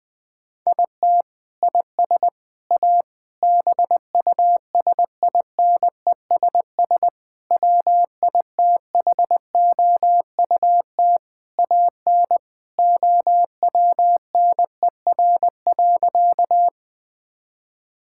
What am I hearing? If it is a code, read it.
Morse code, heard as a keyed tone, about 20 wpm, 700 Hz. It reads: IT IS A BUSINESS WITHOUT AN OWNER.
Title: Sentences from Top 1600 Words 20wpm